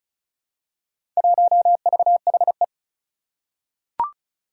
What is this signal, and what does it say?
Morse code, heard as a keyed tone, about 36 wpm, 700 Hz. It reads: 1VHE
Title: Four Challenging Letter-Number Combinations 35wpm